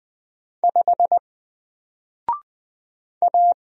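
Morse code, heard as a keyed tone, about 20 wpm, 700 Hz. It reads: 5 A
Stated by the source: Single letter and number Morse drill at 20 wpm, from N4LCD.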